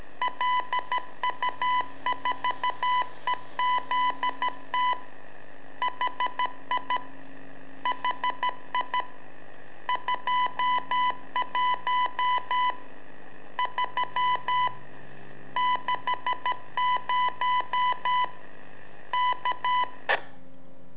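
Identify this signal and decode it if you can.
Morse code, heard as a keyed tone, about 13 wpm, 1000 Hz. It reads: LU4EZT HI HI 21 3 60 K
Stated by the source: Telemetry in Morse from LU7AA and LU4EZ, AMSAT and Turma Mar del Plata, sent from a balloon.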